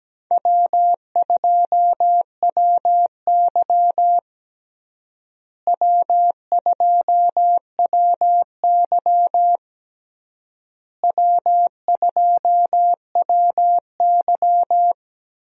Morse code, heard as a keyed tone, about 17 wpm, 700 Hz. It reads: W2WY W2WY W2WY
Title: USA Call Signs - 4 Characters - Repeated 3x - 17wpm